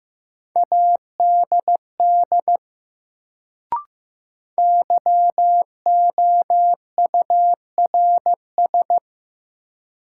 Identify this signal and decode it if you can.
Morse code, heard as a keyed tone, about 15 wpm, 700 Hz. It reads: ADD YOURS
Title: Sentences from Top 1500 Words - New Word Drill Alt - 15wpm